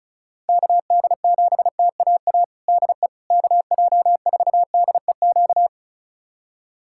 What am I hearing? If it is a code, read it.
Morse code, heard as a keyed tone, about 34 wpm, 700 Hz. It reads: KD7TAA DE KJ4DEQ
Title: USA Callsign DE Callsign 35wpm